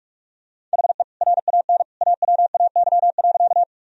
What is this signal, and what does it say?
Morse code, heard as a keyed tone, about 45 wpm, 700 Hz. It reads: HE RAN AWAY.
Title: Sentences from Top 1000 Words - Review Part 3 - 45wpm